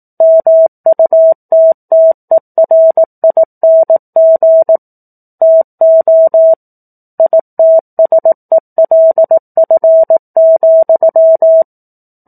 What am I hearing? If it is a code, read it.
Morse code, heard as a keyed tone, about 18 wpm, 650 Hz. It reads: MUTTERING TO ITSELF,
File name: 04-Chapter-04 18 WPM